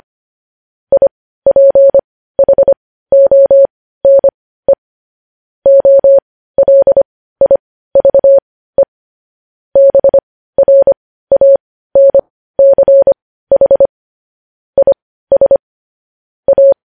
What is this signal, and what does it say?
Morse code, heard as a keyed tone, about 25 wpm, spaced as for 15 wpm, 550 Hz. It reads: IPHONE OLIVE BRANCH IS A
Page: news-headlines-in-morse-code-at-15-wpm